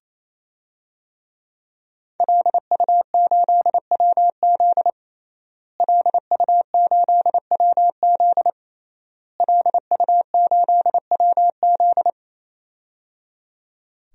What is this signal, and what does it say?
Morse code, heard as a keyed tone, about 28 wpm, 700 Hz. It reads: LU8WZ LU8WZ LU8WZ
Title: DXCC-100 Call Signs - 5 Characters - Repeated 3x - 28wpm